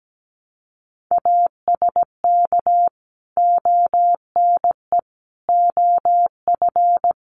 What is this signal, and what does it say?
Morse code, heard as a keyed tone, about 17 wpm, 700 Hz. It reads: ASK ONE OF
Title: Sentences from Top 300 Words 17wpm